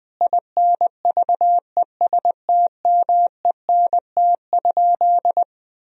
Morse code, heard as a keyed tone, about 20 wpm, 700 Hz. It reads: INVESTMENT?
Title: Sentences from Top 1000 Words - Review Part 4 - 20wpm